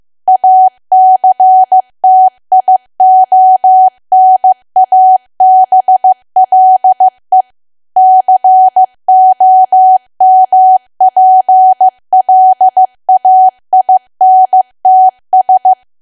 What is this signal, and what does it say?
Morse code, heard as a keyed tone, about 15 wpm, 750 Hz. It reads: ACTIONABLE COMPLAINTS
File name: cwa / W1AW-Level2 / 15WPM